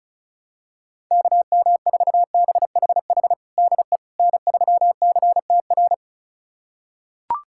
Morse code, heard as a keyed tone, about 36 wpm, 700 Hz. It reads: KM4BHH DE N3CTR
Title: USA Callsign DE Callsign 35wpm